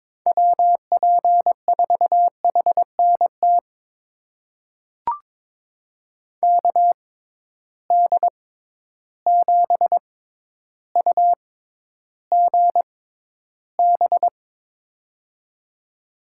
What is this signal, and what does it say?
Morse code, heard as a keyed tone, about 22 wpm, spaced as for 11 wpm, 700 Hz. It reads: WP4HNT KD7UGB
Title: USA Call Signs - 6 Characters - 6x Character Spacing - 22wpm